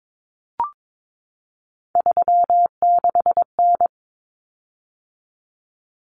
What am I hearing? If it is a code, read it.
Morse code, heard as a keyed tone, about 22 wpm, 700 Hz. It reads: 36N
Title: Common Cut Numbers - 3-Digit 22wpm